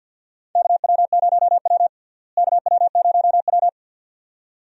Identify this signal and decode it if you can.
Morse code, heard as a keyed tone, about 52 wpm, 700 Hz. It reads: KW0W KW0W
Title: USA Call Signs - 4 Characters - Repeated 2x - 50wpm